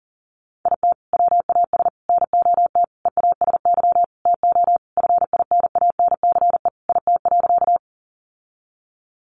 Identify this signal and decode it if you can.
Morse code, heard as a keyed tone, about 40 wpm, 700 Hz. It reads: IT WAS NOT EASY TO FINANCE IT.
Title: Sentences from Top 1500 Words 40wpm